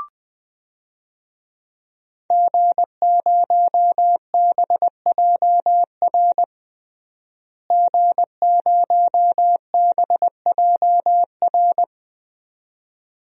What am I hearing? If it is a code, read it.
Morse code, heard as a keyed tone, about 21 wpm, 700 Hz. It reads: G0BJR G0BJR
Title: DXCC-100 Call Signs - 5 Characters - Repeated 2x - 20wpm